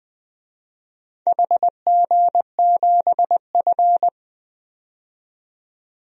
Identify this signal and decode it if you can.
Morse code, heard as a keyed tone, about 20 wpm, 700 Hz. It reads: HG7F
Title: DXCC-100 Call Signs - 4 Characters - 20wpm